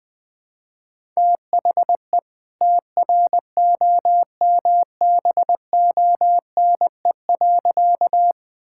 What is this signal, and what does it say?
Morse code, heard as a keyed tone, about 20 wpm, 700 Hz. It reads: THE TROMBONE.
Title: All-Age Appropriate Jokes - 20wpm